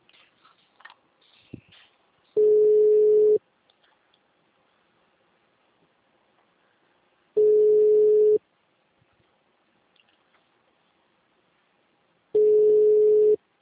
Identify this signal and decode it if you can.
Ringback tone (Europe, 425 Hz).